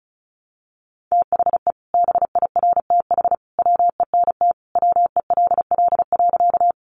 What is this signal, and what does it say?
Morse code, heard as a keyed tone, about 34 wpm, 700 Hz. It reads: THE BIRTH WENT WELL.
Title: Sentences from Top 800 Words 35wpm